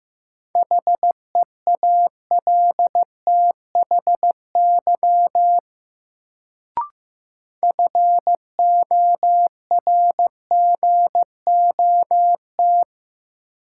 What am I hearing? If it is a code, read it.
Morse code, heard as a keyed tone, about 15 wpm, 700 Hz. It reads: HEALTHY FORGOT
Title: Sentences from Top 1400 Words - New Word Drill Alt - 15wpm